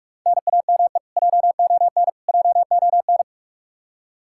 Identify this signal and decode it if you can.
Morse code, heard as a keyed tone, about 46 wpm, 700 Hz. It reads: NAME JON JON
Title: K1USN SST QSO Practice 45wpm